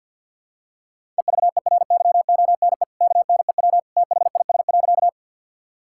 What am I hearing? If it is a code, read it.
Morse code, heard as a keyed tone, about 52 wpm, 700 Hz. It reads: EVERYONE KNEW THIS.
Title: Sentences from Top 1000 Words - Review Part 5 - 50wpm